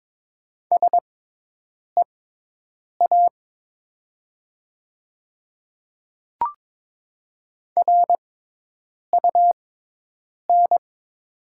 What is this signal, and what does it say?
Morse code, heard as a keyed tone, about 22 wpm, spaced as for 8 wpm, 700 Hz. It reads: SEA RUN
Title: 6x Character Spacing - Top 300 Words - 22wpm